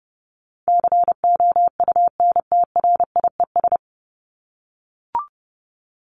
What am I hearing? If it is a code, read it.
Morse code, heard as a keyed tone, about 29 wpm, 700 Hz. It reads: COUNTRIES
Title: ES:S Suffix - Top 100 Words with Base Words 30wpm